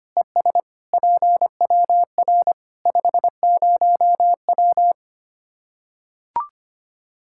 Morse code, heard as a keyed tone, about 25 wpm, 700 Hz. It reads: ES PWR 50W
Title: Practice QSO - Exchange 2 - Info - 25wpm